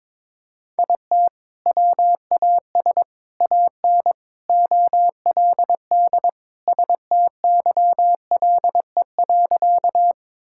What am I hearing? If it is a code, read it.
Morse code, heard as a keyed tone, about 22 wpm, 700 Hz. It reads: IT WAS AN OLD STYLE.
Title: Sentences from Top 1000 Words - Review Part 5 - 22wpm